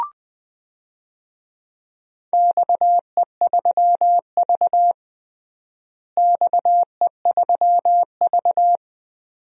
Morse code, heard as a keyed tone, about 21 wpm, 700 Hz. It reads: XE3V XE3V
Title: DXCC-100 Call Signs - 4 Characters - Repeated 2x - 20wpm